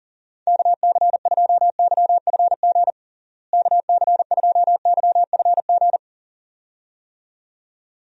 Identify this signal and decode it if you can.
Morse code, heard as a keyed tone, about 40 wpm, 700 Hz. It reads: KC2YFG KC2YFG
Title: USA Call Signs - 6 Characters - Repeated 2x - 40wpm